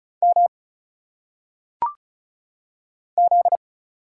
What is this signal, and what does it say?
Morse code, heard as a keyed tone, about 36 wpm, 700 Hz. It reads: M Z
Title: Single Letters - Phonetic Pronunciation 35wpm